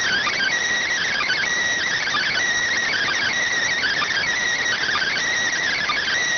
Emergency Alert System (SAME) tones (header bursts), heard as harsh buzzing bursts. It is heard from 0.9 to 5.3 s.